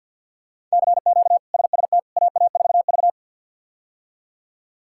Morse code, heard as a keyed tone, about 46 wpm, 700 Hz. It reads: CQ SST AA4V
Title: K1USN SST QSO Practice - 50wpm